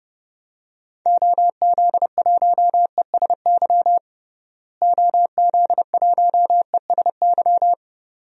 Morse code, heard as a keyed tone, about 32 wpm, 700 Hz. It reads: OZ1ESY OZ1ESY
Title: DXCC-100 Call Signs - 6 Characters - Repeated 2x - 30wpm